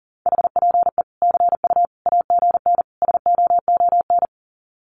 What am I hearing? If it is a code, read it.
Morse code, heard as a keyed tone, about 40 wpm, 700 Hz. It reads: HPE CU AGN SOON
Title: Common QSO Phrases 40wpm